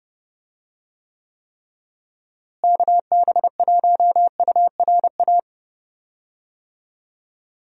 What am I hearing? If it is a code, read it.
Morse code, heard as a keyed tone, about 30 wpm, 700 Hz. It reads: KB1URA
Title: USA Call Signs - 6 Characters - Repeated 3x - 30wpm